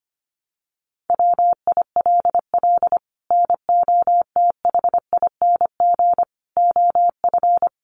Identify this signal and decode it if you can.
Morse code, heard as a keyed tone, about 25 wpm, 700 Hz. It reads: WILL NOTHING OF
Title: Sentences from Top 1200 Words 25wpm